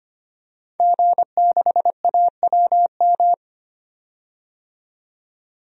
Morse code, heard as a keyed tone, about 25 wpm, 700 Hz. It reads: G6AWM
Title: DXCC-100 Call Signs - 5 Characters - 25wpm